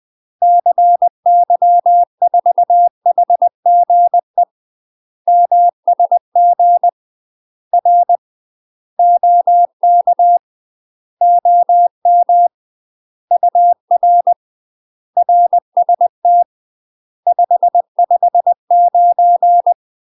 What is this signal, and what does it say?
Morse code, heard as a keyed tone, about 20 wpm, 700 Hz. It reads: CY4HGE MSG R OK OM UR RST 559
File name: UK MASTER QSOs TRAINING / 20WPM / UK MASTER QSOs 0003